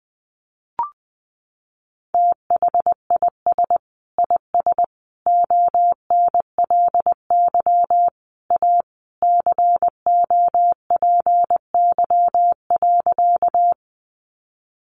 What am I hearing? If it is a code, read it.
Morse code, heard as a keyed tone, about 19 wpm, 700 Hz. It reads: THIS IS ONLY A COPY.
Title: Sentences from Top 1400 Words 20wpm